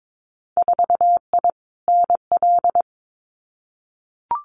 Morse code, heard as a keyed tone, about 22 wpm, 700 Hz. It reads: 4I NL
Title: Winter Field Day Exchanges Encore 22wpm